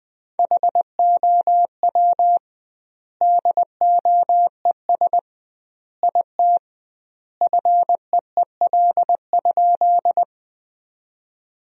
Morse code, heard as a keyed tone, about 21 wpm, 700 Hz. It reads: HOW DOES IT FEEL?
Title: 2x Word-Spacing - Sentences from Top 400 Words - 20wpm